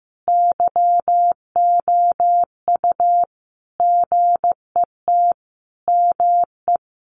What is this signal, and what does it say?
Morse code, heard as a keyed tone, about 15 wpm, 700 Hz. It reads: YOU GET ME
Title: Sentences from Top 200 Words 15wpm